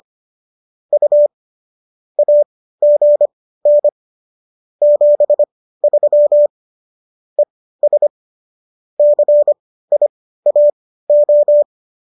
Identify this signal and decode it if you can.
Morse code, heard as a keyed tone, about 25 wpm, spaced as for 15 wpm, 590 Hz. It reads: U AGN 73 ES CIAO